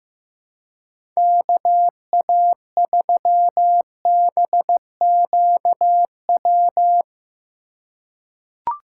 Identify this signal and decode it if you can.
Morse code, heard as a keyed tone, about 15 wpm, 700 Hz. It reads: KA3BQW
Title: USA Call Signs - 6 Characters - 2x Character Spacing - 15wpm